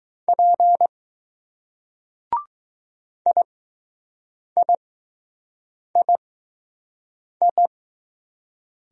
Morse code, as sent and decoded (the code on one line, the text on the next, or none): .--. / .. .. .. ..
P IIII